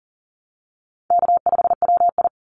Morse code, heard as a keyed tone, about 40 wpm, 700 Hz. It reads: K5WI
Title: NA SOTA Chaser Callsigns 40wpm